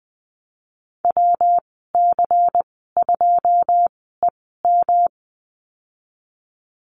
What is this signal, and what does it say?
Morse code, heard as a keyed tone, about 20 wpm, spaced as for 15 wpm, 700 Hz. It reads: WC2EM
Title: 161 Lesson 26g - Letter g - Call Signs 20wpm